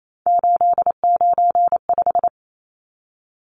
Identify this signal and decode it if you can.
Morse code, heard as a keyed tone, about 28 wpm, 700 Hz. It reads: 895
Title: Numbers - 3-Digit 28wpm